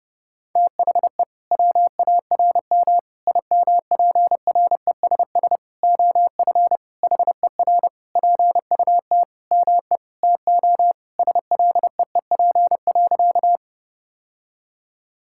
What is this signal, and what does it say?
Morse code, heard as a keyed tone, about 30 wpm, 700 Hz. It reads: THE WARM IMPRESS OF HER PUT ME TO SLEEP.
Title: Sentences from Top 1600 Words 30wpm